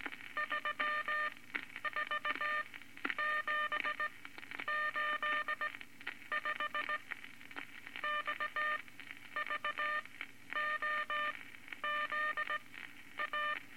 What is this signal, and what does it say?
Morse code, heard as a keyed tone, about 18 wpm, spaced as for 10 wpm, 1200 Hz, with noise. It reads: 34785 XVOZA
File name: Morse Code Records / 19 A LCM 166 Ten Words Per Minute